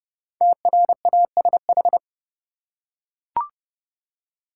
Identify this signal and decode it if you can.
Morse code, heard as a keyed tone, about 30 wpm, 700 Hz. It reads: TRASH
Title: Five Letter Words - 30wpm